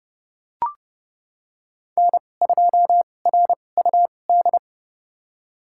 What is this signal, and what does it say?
Morse code, heard as a keyed tone, about 30 wpm, spaced as for 22 wpm, 700 Hz. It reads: N2RUD
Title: USA Call Signs - 5 Characters - 2x Character Spacing - 30wpm